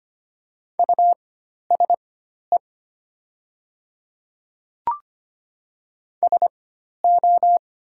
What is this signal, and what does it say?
Morse code, heard as a keyed tone, about 25 wpm, spaced as for 12 wpm, 700 Hz. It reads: USE SO
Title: 4x Character Spacing - Top 100 Words - 25wpm